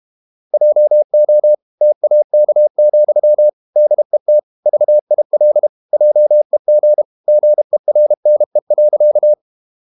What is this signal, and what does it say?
Morse code, heard as a keyed tone, about 32 wpm, 600 Hz. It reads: JO TAK, DET VIL JEG GERNE.
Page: Ferie 32 wpm